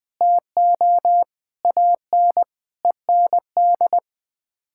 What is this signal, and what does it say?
Morse code, heard as a keyed tone, about 21 wpm, 700 Hz. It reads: TO AN END
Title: Sets of 8 Words - 20wpm